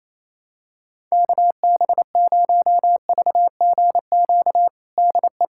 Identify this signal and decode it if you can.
Morse code, heard as a keyed tone, about 28 wpm, 700 Hz. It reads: KB0VGQ DE
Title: USA Callsign DE Callsign 28wpm